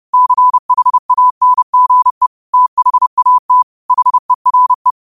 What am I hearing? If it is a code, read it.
Morse code, heard as a keyed tone, about 30 wpm, 1000 Hz. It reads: CHANGE THAT HERE
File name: qotdfiles / 30wpm